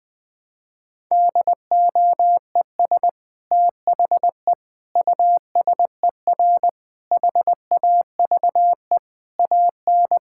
Morse code, heard as a keyed tone, about 20 wpm, 700 Hz. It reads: DOES THE USER HAVE AN
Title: Sentences from Top 1000 Words - Review Part 5 - 20wpm